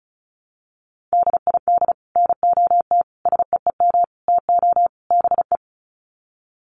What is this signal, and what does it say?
Morse code, heard as a keyed tone, about 34 wpm, 700 Hz. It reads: DID NOT SEEM TO BE
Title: Sets of 5 Words Encore 35wpm